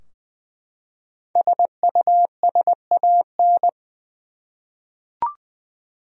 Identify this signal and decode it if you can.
Morse code, heard as a keyed tone, about 21 wpm, 700 Hz. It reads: SUSAN
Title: USA Common Names - Max 5 Letters - 20wpm